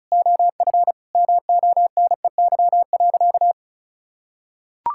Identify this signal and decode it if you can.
Morse code, heard as a keyed tone, about 34 wpm, 700 Hz. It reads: OF MONEY.
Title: Sentences from Top 1000 Words - Review Part 5 - 35wpm